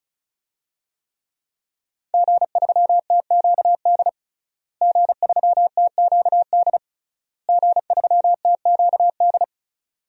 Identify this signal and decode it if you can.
Morse code, heard as a keyed tone, about 36 wpm, 700 Hz. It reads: G3TQD G3TQD G3TQD
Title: DXCC-100 Call Signs - 5 Characters - Repeated 3x - 35wpm